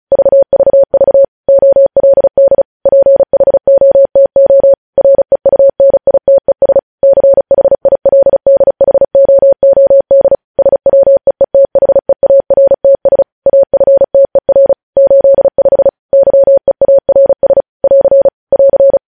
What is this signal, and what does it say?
Morse code, heard as a keyed tone, about 36 wpm, 550 Hz. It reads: VVV OLD PHOTO REUNITES CHILDHOOD SWEETHEARTS AFTER 85 YEARS <AR> <AR>
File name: podcast / News headlines-1x-35x35